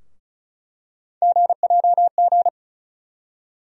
Morse code, heard as a keyed tone, about 34 wpm, 700 Hz. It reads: GJG
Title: USA Call Sign Postfixes - 3 Letters 35wpm